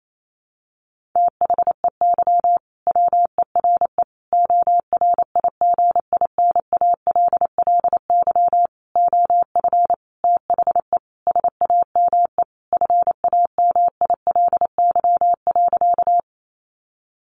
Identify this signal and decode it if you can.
Morse code, heard as a keyed tone, about 28 wpm, 700 Hz. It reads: THEY WERE ORIGINALLY OF THE SAME FAMILY.
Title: Sentences from Top 1500 Words 28wpm